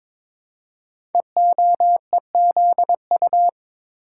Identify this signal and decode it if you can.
Morse code, heard as a keyed tone, about 22 wpm, 700 Hz. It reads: EOEZU
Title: Code Groups - 5 Letters - 22wpm